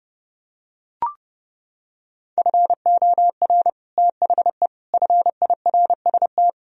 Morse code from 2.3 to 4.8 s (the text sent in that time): FOR THE